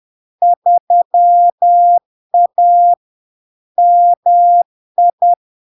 Morse code, heard as a keyed tone, about 10 wpm, 700 Hz. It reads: 3A MI
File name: LARK-FD-2022-Log-in-CW-10wpm-700-Hz